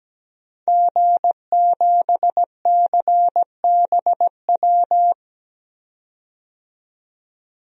Morse code, heard as a keyed tone, about 17 wpm, 700 Hz. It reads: G7CBW
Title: DXCC-100 Call Signs - 5 Characters - 17wpm